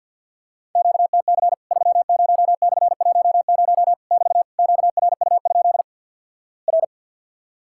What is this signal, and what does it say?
Morse code, heard as a keyed tone, about 54 wpm, 700 Hz. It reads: QTC 30/10 <BT> QRU? R
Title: Worked All Europe DX Contest - QTC Exchange with Repeats - 50wpm